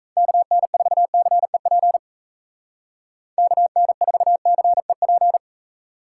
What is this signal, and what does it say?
Morse code, heard as a keyed tone, about 37 wpm, 700 Hz. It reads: KN4CEP KN4CEP